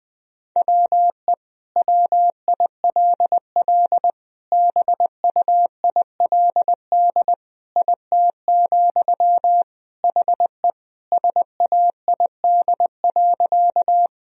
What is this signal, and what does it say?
Morse code, heard as a keyed tone, about 21 wpm, 700 Hz. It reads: WE WILL BUILD IT, HE SAID.
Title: Sentences from Top 1000 Words - Review Part 1 - 20wpm